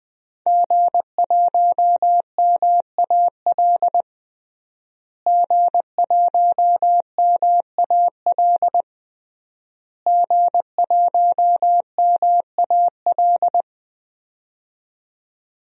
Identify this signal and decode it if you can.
Morse code, heard as a keyed tone, about 20 wpm, 700 Hz. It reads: G1MAL G1MAL G1MAL
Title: DXCC-100 Call Signs - 5 Characters - Repeated 3x - 20wpm